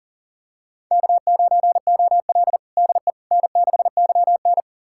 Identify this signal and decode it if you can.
Morse code, heard as a keyed tone, about 37 wpm, 700 Hz. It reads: K9OL DE N6YN
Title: USA Callsign DE Callsign - 4 Characters - 40wpm